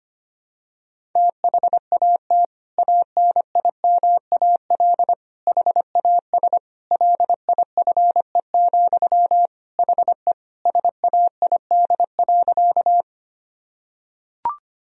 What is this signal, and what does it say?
Morse code, heard as a keyed tone, about 25 wpm, 700 Hz. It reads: THAT ANIMAL HAS LIFE, HE SAID.
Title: Sentences from Top 1000 Words - Review Part 1 - 25wpm